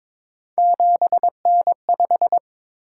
Morse code, heard as a keyed tone, about 22 wpm, 700 Hz. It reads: 7N5
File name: practice / 22wpm / Three Letter-Number Combinations - 22wpm